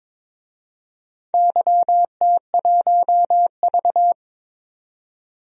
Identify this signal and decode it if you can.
Morse code, heard as a keyed tone, about 22 wpm, 700 Hz. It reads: YT1V
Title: DXCC-100 Call Signs - 4 Characters - 22wpm